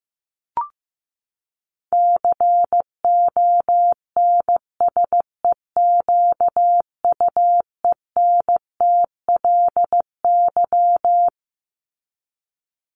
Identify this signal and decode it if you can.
Morse code, heard as a keyed tone, about 15 wpm, 700 Hz. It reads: CONSEQUENTLY
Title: LY Suffix - Top 100 Words with Base Words 15wpm